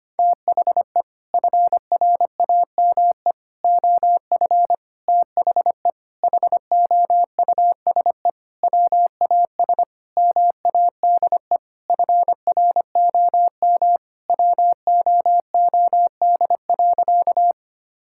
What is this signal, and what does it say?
Morse code, heard as a keyed tone, about 25 wpm, 700 Hz. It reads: THE FRAME OF THE HOUSE WAS MADE FROM WOOD.
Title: Sentences from Top 1000 Words - Review Part 6 - 25wpm